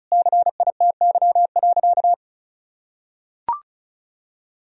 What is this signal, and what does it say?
Morse code, heard as a keyed tone, about 36 wpm, 700 Hz. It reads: CITY.
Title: Sentences from Top 1000 Words - Review Part 1 - 35wpm